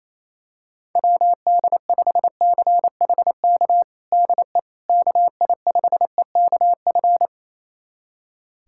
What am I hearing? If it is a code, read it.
Morse code, heard as a keyed tone, about 28 wpm, 700 Hz. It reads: WD5CHK DE KI5EKF
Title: USA Callsign DE Callsign 28wpm